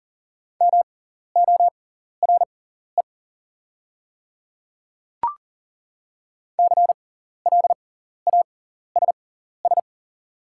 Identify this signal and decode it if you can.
Morse code, heard as a keyed tone, about 37 wpm, spaced as for 14 wpm, 700 Hz. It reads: MORE CLASS
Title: 6x Character Spacing - Top 500 Words - 40wpm